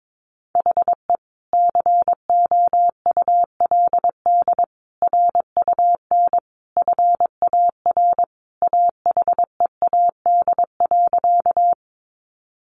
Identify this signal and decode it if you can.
Morse code, heard as a keyed tone, about 22 wpm, 700 Hz. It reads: HE COULD RUN FAR AHEAD.
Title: Sentences from Top 1100 Words 22wpm